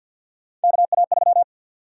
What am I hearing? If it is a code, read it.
Morse code, heard as a keyed tone, about 52 wpm, 700 Hz. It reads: KA3
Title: USA Call Sign Prefixes - 50wpm